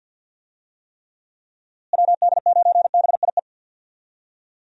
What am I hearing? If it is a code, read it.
Morse code, heard as a keyed tone, about 50 wpm, 700 Hz. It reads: WD9BIE
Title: USA Call Signs - 6 Characters - Repeated 3x - 50wpm